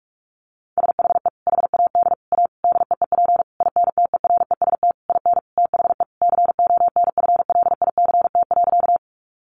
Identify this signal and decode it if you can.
Morse code, heard as a keyed tone, about 44 wpm, 700 Hz. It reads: SHE HAD A DEEP INTEREST IN THE CONFLICT.
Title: Sentences from Top 1300 Words 45wpm